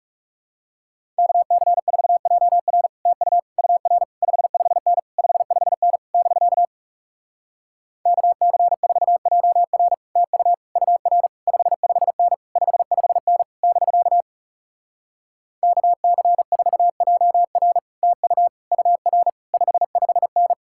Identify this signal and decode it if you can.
Morse code, heard as a keyed tone, about 40 wpm, 700 Hz. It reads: KC4JR TU UR 55N 55N <BK> KC4JR TU UR 55N 55N <BK> KC4JR TU UR 55N